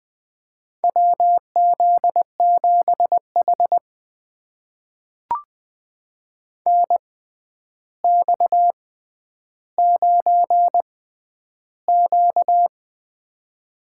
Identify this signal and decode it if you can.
Morse code, heard as a keyed tone, about 21 wpm, spaced as for 10 wpm, 700 Hz. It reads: WZ7H NX9Q
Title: USA Call Signs - 4 Characters - 6x Character Spacing - 20wpm